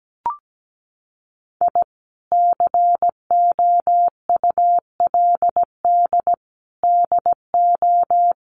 Morse code, heard as a keyed tone, about 17 wpm, 700 Hz. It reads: I COULD DO
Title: Sentences from Top 1300 Words 17wpm